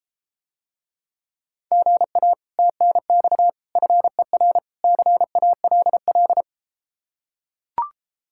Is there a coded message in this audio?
Yes: Morse code from 1.7 to 6.4 s, heard as a keyed tone, about 33 wpm, 700 Hz. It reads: GA TNX FER CALL